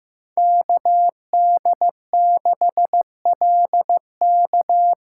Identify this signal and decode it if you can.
Morse code, heard as a keyed tone, about 15 wpm, 700 Hz. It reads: KD6LK
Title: USA Call Signs - 2x Character Spacing - 15wpm